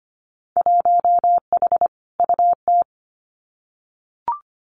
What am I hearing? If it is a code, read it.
Morse code, heard as a keyed tone, about 25 wpm, 700 Hz. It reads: 1H UT
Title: Winter Field Day Exchanges 25wpm